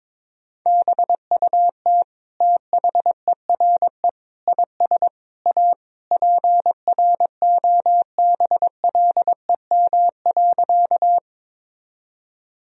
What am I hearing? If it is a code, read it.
Morse code, heard as a keyed tone, about 21 wpm, 700 Hz. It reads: BUT THERE IS A PROBLEM.
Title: Sentences from Top 1000 Words - Review Part 2 - 22wpm